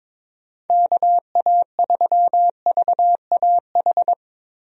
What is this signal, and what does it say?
Morse code, heard as a keyed tone, about 22 wpm, 700 Hz. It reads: KA3VAH
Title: USA Call Signs - 6 Characters - Repeated 2x - 22wpm